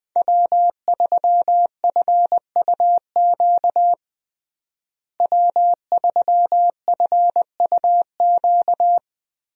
Morse code, heard as a keyed tone, about 20 wpm, 700 Hz. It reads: W3FUQ W3FUQ
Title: USA Call Signs - 5 Characters - Repeated 2x - 20wpm